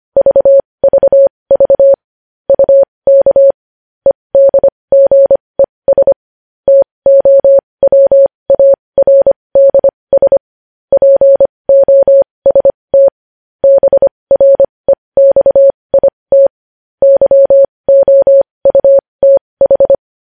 Morse code, heard as a keyed tone, about 25 wpm, spaced as for 20 wpm, 550 Hz. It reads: VVV UK EDGES TOWARDS POST BREXIT YOUTH
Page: News Headlines in Morse Code at 20 WPM